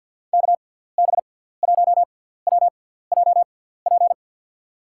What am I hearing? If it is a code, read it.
Morse code, heard as a keyed tone, about 53 wpm, spaced as for 17 wpm, 700 Hz. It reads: KB1WJP
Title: USA Call Signs - 6 Characters - 6x Character Spacing - 50wpm